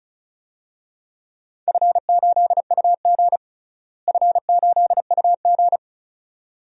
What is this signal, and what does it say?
Morse code, heard as a keyed tone, about 36 wpm, 700 Hz. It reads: F8UG F8UG
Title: DXCC-100 Call Signs - 4 Characters - Repeated 2x - 35wpm